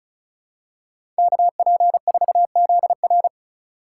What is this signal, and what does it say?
Morse code, heard as a keyed tone, about 34 wpm, 700 Hz. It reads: KP4ZR
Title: USA Call Signs - 5 Characters - 4x Character Spacing - 35wpm